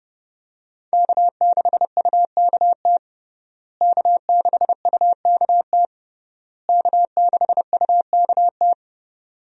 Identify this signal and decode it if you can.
Morse code, heard as a keyed tone, about 32 wpm, 700 Hz. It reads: K6UKT K6UKT K6UKT
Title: USA Call Signs - 5 Characters - Repeated 3x - 30wpm